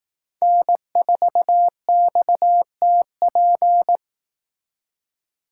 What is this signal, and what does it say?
Morse code, heard as a keyed tone, about 18 wpm, 700 Hz. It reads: N4XTP
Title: Speed Racing - USA Call Signs 15wpm